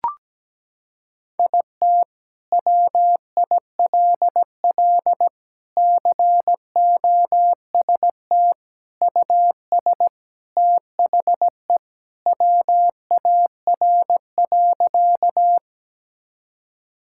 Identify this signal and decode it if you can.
Morse code, heard as a keyed tone, about 17 wpm, 700 Hz. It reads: IT WILL COST US THE WAR.
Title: Sentences from Top 1000 Words - Review Part 5 - 17wpm